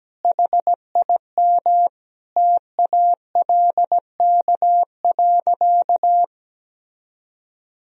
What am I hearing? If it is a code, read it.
Morse code, heard as a keyed tone, about 17 wpm, 700 Hz. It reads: HIM TALK.
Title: Sentences from Top 1000 Words - Review Part 4 - 17wpm